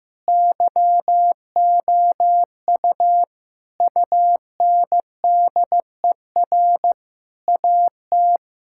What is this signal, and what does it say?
Morse code, heard as a keyed tone, about 15 wpm, 700 Hz. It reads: YOU UNDER AT
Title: Sentences from Top 1000 Words - Review Part 6 - 15wpm